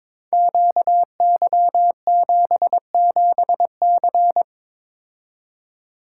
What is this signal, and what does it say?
Morse code, heard as a keyed tone, about 22 wpm, 700 Hz. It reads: QY77C
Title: Code Groups - 5 Letter:Numbers - 22wpm